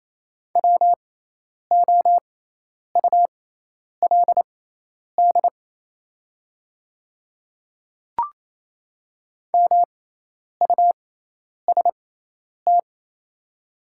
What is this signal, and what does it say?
Morse code, heard as a keyed tone, about 28 wpm, spaced as for 10 wpm, 700 Hz. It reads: WOULD MUST
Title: 6x Character Spacing - Top 300 Words - 28wpm